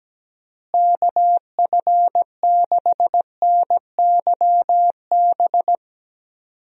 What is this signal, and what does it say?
Morse code, heard as a keyed tone, about 17 wpm, 700 Hz. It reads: KF6NYB
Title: USA Call Signs - 6 Characters - Repeated 3x - 17wpm